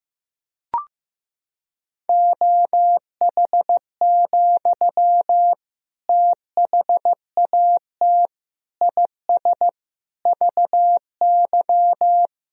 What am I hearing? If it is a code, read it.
Morse code, heard as a keyed tone, about 15 wpm, 700 Hz. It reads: OH, THAT IS VY
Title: Sentences from Top 500 Words and Abbreviations 15wpm